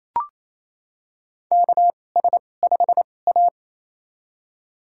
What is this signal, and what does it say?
Morse code, heard as a keyed tone, about 28 wpm, spaced as for 20 wpm, 690 Hz. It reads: KS5A
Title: USA Call Signs - 4 Characters - 2x Character Spacing - 28wpm